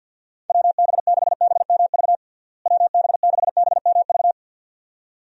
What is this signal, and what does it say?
Morse code, heard as a keyed tone, about 51 wpm, 700 Hz. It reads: WB6BMV WB6BMV
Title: USA Call Signs - 6 Characters - Repeated 2x - 50wpm